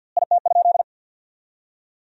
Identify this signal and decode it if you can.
Morse code, heard as a keyed tone, about 52 wpm, 700 Hz. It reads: IT?
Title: 2x Word-Spacing - Sentences from Top 100 Words - 50wpm